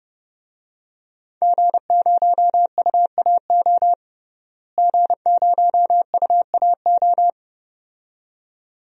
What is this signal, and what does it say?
Morse code, heard as a keyed tone, about 30 wpm, 700 Hz. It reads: G0UAO G0UAO
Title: DXCC-100 Call Signs - 5 Characters - Repeated 2x - 30wpm